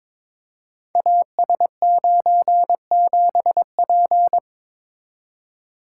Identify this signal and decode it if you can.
Morse code, heard as a keyed tone, about 22 wpm, 700 Hz. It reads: AS97P
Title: Code Groups - 5 Letter:Numbers - 22wpm